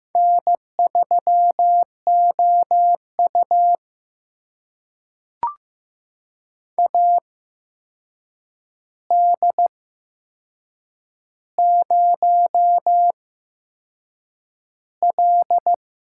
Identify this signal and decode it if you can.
Morse code, heard as a keyed tone, about 15 wpm, spaced as for 6 wpm, 700 Hz. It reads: N3OU AD0L
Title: USA Call Signs - 4 Characters - 8x Character Spacing - 15wpm